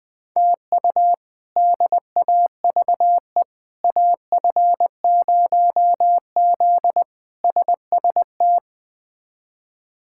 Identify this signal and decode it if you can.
Morse code, heard as a keyed tone, about 20 wpm, 700 Hz. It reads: TU DAVE AF0Z SST
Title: K1USN SST QSO Practice - 20wpm